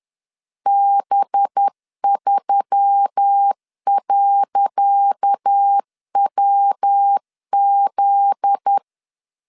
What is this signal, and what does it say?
Morse code, heard as a keyed tone, about 10 wpm, 800 Hz. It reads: B3.WZ
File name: LEZIONI / 10 SIGNS / 10C DOT 10wpm 2940